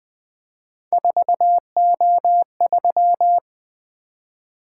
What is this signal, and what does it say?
Morse code, heard as a keyed tone, about 20 wpm, 700 Hz. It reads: 4O3
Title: Three Letter-Number Combinations - 20wpm